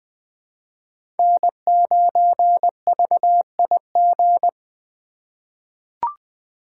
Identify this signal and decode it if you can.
Morse code, heard as a keyed tone, about 20 wpm, 700 Hz. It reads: N9VIG
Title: USA Call Signs - 3x Character Spacing - 20wpm